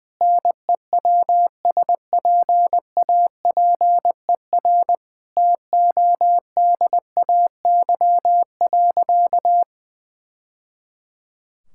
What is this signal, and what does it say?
Morse code, heard as a keyed tone, about 20 wpm, 700 Hz. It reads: NEWSPAPER TODAY.
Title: Sentences from Top 1000 Words - Review Part 6 - 20wpm